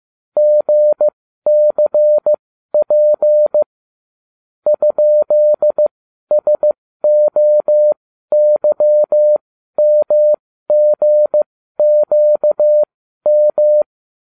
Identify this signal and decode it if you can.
Morse code, heard as a keyed tone, about 15 wpm, spaced as for 12 wpm, 600 Hz. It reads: GCP ?SOYMGQM